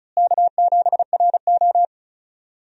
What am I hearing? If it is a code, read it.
Morse code, heard as a keyed tone, about 34 wpm, 700 Hz. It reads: K7RO
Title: USA Call Signs - 4 Characters - Repeated 2x - 35wpm